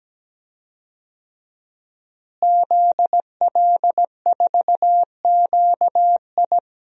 Morse code, heard as a keyed tone, about 17 wpm, 700 Hz. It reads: ZL4QI
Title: DXCC-100 Call Signs - 5 Characters - Repeated 3x - 17wpm